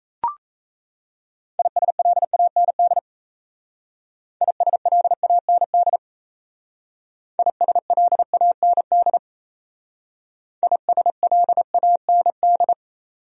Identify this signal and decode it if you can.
Morse code, heard as a keyed tone, about 32 wpm, 700 Hz. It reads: ISLAND ISLAND ISLAND ISLAND
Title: Speed-Racing Top 500 Words 28wpm